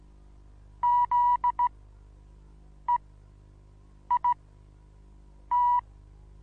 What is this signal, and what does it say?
Morse code, heard as a keyed tone, about 15 wpm, spaced as for 6 wpm, 970 Hz. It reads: ZEIT
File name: Onlinelehrgang / morsen / DJ4UF Morsen25